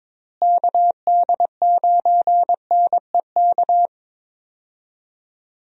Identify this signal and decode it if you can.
Morse code, heard as a keyed tone, about 21 wpm, 700 Hz. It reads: KD9NEK
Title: USA Call Signs - 6 Characters - Repeated 3x - 22wpm